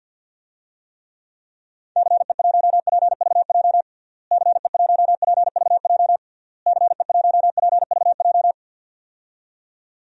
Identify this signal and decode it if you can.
Morse code, heard as a keyed tone, about 50 wpm, 700 Hz. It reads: CE1PVJ CE1PVJ CE1PVJ
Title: DXCC-100 Call Signs - 6 Characters - Repeated 3x - 50wpm